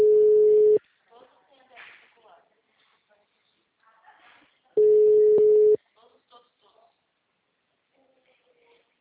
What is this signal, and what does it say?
Ringback tone (Europe, 425 Hz), with speech.